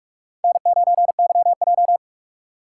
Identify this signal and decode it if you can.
Morse code, heard as a keyed tone, about 46 wpm, 700 Hz. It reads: N9YJ
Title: USA Call Signs - 4 Characters - 3x Character Spacing - 45wpm